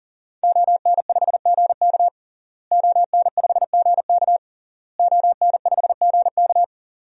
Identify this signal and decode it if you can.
Morse code, heard as a keyed tone, about 40 wpm, 700 Hz. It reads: ON5GK ON5GK ON5GK
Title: DXCC-100 Call Signs - 5 Characters - Repeated 3x - 40wpm